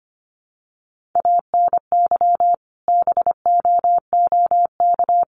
Morse code, heard as a keyed tone, about 25 wpm, 700 Hz. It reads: ANY BOOK
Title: Sentences from Top 300 Words 25wpm